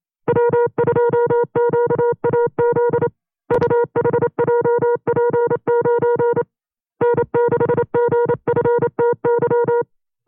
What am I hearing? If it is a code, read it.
Morse code, heard as a keyed tone, about 28 wpm, 480 Hz. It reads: W2QAZ UHJP9 N6GFTY
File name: FreeFiles / CW-practice / 25WPM-side1